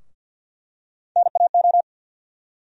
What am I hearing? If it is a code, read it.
Morse code, heard as a keyed tone, about 52 wpm, 700 Hz. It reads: NAO
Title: USA Call Sign Postfixes - 3 Letters 50wpm